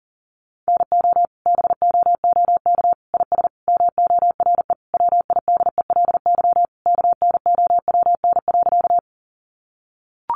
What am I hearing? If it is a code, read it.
Morse code, heard as a keyed tone, about 40 wpm, 700 Hz. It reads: NO BOOK IS MORE WIDELY KNOWN.
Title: Sentences from Top 1200 Words 40wpm